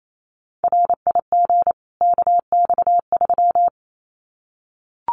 Morse code, heard as a keyed tone, about 28 wpm, 700 Hz. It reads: RIG KX3
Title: Common QSO Phrases 28wpm